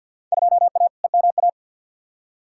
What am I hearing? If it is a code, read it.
Morse code, heard as a keyed tone, about 53 wpm, 700 Hz. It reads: 2A EMA
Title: Field Day Exchanges - 50wpm